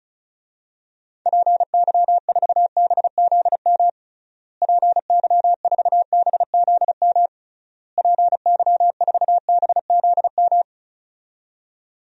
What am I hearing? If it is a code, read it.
Morse code, heard as a keyed tone, about 34 wpm, 700 Hz. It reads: PY4BZM PY4BZM PY4BZM
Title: DXCC-100 Call Signs - 6 Characters - Repeated 3x - 35wpm